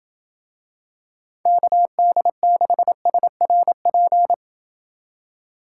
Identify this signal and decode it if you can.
Morse code, heard as a keyed tone, about 28 wpm, 700 Hz. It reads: KD6SRP